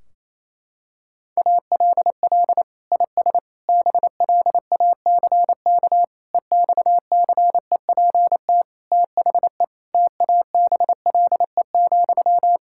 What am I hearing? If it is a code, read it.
Morse code, heard as a keyed tone, about 27 wpm, 700 Hz. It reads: ALL IS BLACK EXCEPT THE TABLE,
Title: Sentences from Top 1600 Words 28wpm